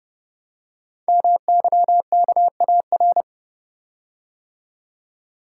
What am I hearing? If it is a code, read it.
Morse code, heard as a keyed tone, about 29 wpm, 700 Hz. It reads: MYKAR
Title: Code Groups - 5 Letters - 30wpm